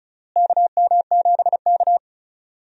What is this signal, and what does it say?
Morse code, heard as a keyed tone, about 36 wpm, 700 Hz. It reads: KM7K
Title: USA Call Signs - 4 Characters - Repeated 3x - 35wpm